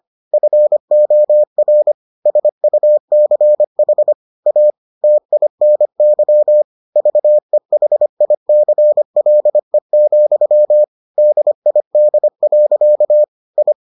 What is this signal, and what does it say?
Morse code, heard as a keyed tone, about 25 wpm, 600 Hz. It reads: FOR SUCH A TINY VEHICLE, DID. I